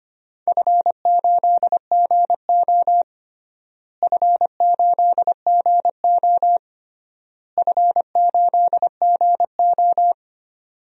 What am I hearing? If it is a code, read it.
Morse code, heard as a keyed tone, about 25 wpm, 700 Hz. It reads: F8GO F8GO F8GO
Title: DXCC-100 Call Signs - 4 Characters - Repeated 3x - 25wpm